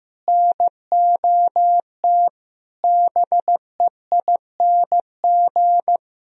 Morse code, heard as a keyed tone, about 15 wpm, 700 Hz. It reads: NOT BEING
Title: Sentences from Top 1000 Words - Review Part 3 - 15wpm